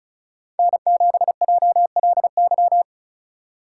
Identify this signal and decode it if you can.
Morse code, heard as a keyed tone, about 34 wpm, 700 Hz. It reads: N7JLY
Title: USA Call Signs - 5 Characters - 4x Character Spacing - 35wpm